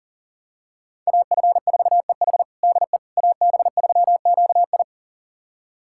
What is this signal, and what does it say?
Morse code, heard as a keyed tone, about 40 wpm, 690 Hz. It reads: AF4EH DE AB3QI
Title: USA Callsign DE Callsign - 5 Characters - 40wpm